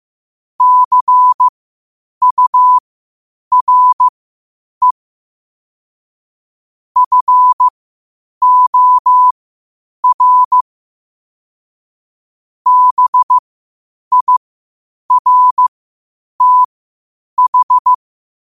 Morse code, heard as a keyed tone, about 15 wpm, spaced as for 8 wpm, 1000 Hz. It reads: CURE FOR BIRTH
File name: qotdfiles / 8wpm